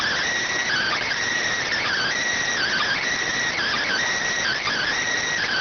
Emergency Alert System (SAME) tones (header bursts), heard as harsh buzzing bursts. It is heard from 0.6 to 4.9 s.